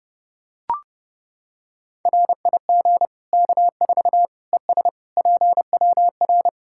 Morse code, heard as a keyed tone, about 32 wpm, 700 Hz. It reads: RIG K4 ES PWR
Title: Practice QSO - Exchange 2 - Info - 30wpm